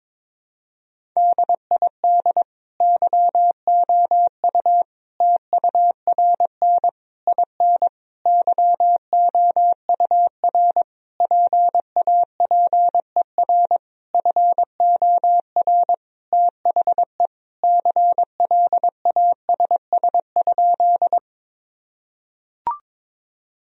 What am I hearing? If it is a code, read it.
Morse code, heard as a keyed tone, about 22 wpm, 700 Hz. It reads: DID YOU TURN IN YOUR PAPER FOR THE CLASS?
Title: Sentences from Top 1000 Words - Review Part 2 - 22wpm